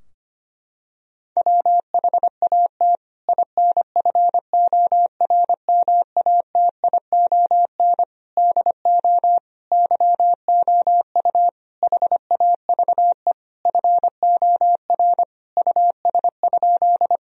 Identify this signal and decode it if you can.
Morse code, heard as a keyed tone, about 25 wpm, 700 Hz. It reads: WHAT INFORMATION DO YOU HAVE FOR US?
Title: Sentences from Top 1000 Words - Review Part 3 - 25wpm